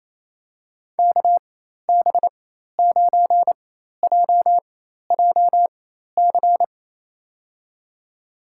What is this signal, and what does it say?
Morse code, heard as a keyed tone, about 28 wpm, spaced as for 13 wpm, 690 Hz. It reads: KB9JJC
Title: USA Call Signs - 6 Characters - 4x Character Spacing - 28wpm